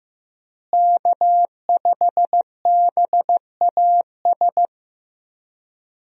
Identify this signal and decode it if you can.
Morse code, heard as a keyed tone, about 15 wpm, 700 Hz. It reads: K5BAS
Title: USA Call Signs - 5 Characters - Repeated 3x - 15wpm